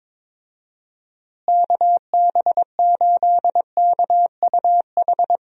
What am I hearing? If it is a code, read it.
Morse code, heard as a keyed tone, about 22 wpm, 700 Hz. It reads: KB8KUH
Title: USA Call Signs - 6 Characters - Repeated 2x - 22wpm